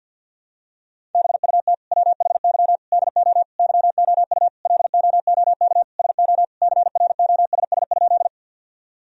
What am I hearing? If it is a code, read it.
Morse code, heard as a keyed tone, about 51 wpm, 700 Hz. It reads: BUT WHY DO YOU LOOK SO CROSS?
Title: Sentences from Top 1000 Words - Review Part 1 - 50wpm